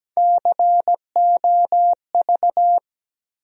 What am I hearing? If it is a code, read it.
Morse code, heard as a keyed tone, about 17 wpm, 700 Hz. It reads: COV
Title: USA Call Sign Postfixes - 3 Letters 17wpm